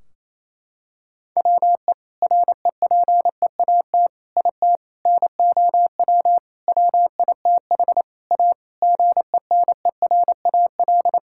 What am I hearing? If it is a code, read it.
Morse code, heard as a keyed tone, about 28 wpm, 700 Hz. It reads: WE REPEAT IT NOW WITH A GENERAL
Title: Sentences from Top 1600 Words 28wpm